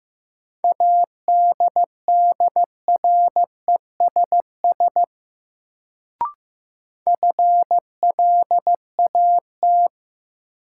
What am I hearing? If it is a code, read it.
Morse code, heard as a keyed tone, about 15 wpm, 700 Hz. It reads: ADDRESS FLAT
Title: Sentences from Top 1000 Words - New Word Drill - 15wpm